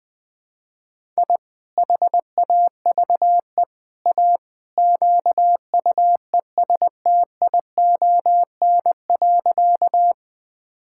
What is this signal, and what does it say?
Morse code, heard as a keyed tone, about 19 wpm, 700 Hz. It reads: I HAVE A QUESTION.
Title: Sentences with 3-Word Phrases - Spoken Beginning - 20wpm